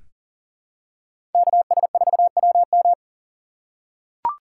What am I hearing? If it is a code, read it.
Morse code, heard as a keyed tone, about 40 wpm, 700 Hz. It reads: KS4WM
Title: USA Call Signs - 2x Character Spacing - 40wpm